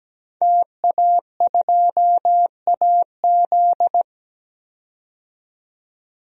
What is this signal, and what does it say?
Morse code, heard as a keyed tone, about 17 wpm, 700 Hz. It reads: TA2AZ
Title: DXCC-100 Call Signs - 5 Characters - 17wpm